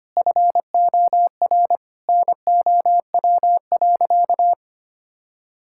Morse code, heard as a keyed tone, about 25 wpm, 700 Hz. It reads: FOR NOW.
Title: Sentences from Top 1000 Words - Review Part 5 - 25wpm